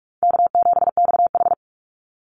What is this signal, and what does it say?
Morse code, heard as a keyed tone, about 44 wpm, 700 Hz. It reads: K7XH
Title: Call Signs Repeated 3x - 45wpm